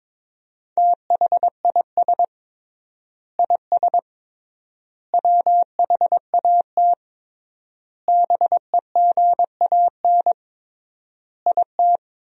Morse code, heard as a keyed tone, about 22 wpm, 700 Hz. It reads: THIS IS WHAT BEGAN IT
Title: 3x Word-Spacing - Sentences from Top 300 Words - 22wpm